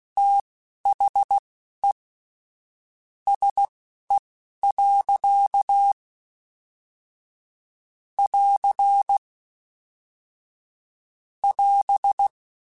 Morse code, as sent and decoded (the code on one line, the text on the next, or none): - .... / ... .-.-.- / .-.-. / .-...
THE SE. <AR> <AS>